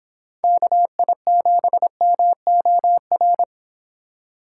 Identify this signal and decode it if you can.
Morse code, heard as a keyed tone, about 25 wpm, 700 Hz. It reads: KI7MOR